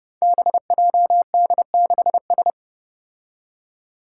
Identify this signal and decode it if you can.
Morse code, heard as a keyed tone, about 28 wpm, 700 Hz. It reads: BJD6S
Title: Five Challenging Letter-Number Combinations 30wpm